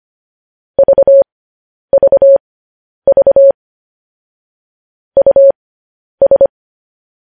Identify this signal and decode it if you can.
Morse code, heard as a keyed tone, about 25 wpm, spaced as for 10 wpm, 550 Hz. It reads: VVV US
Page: News Headlines in Morse Code at 10 WPM